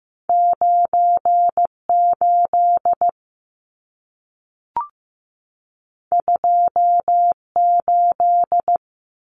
Morse code, heard as a keyed tone, about 15 wpm, 700 Hz. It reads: 98 28
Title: Numbers - 2-Digit 15wpm